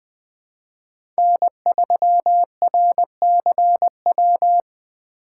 Morse code, heard as a keyed tone, about 20 wpm, 700 Hz. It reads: N3RCW
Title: 2023 POTA Hunter Call Signs - 20wpm